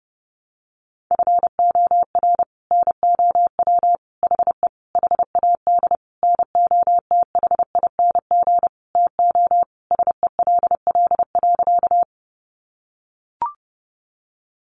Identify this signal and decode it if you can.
Morse code, heard as a keyed tone, about 28 wpm, 700 Hz. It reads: FOR NOW HE HAD NOTHING TO SELL.
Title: Sentences from Top 1100 Words 30wpm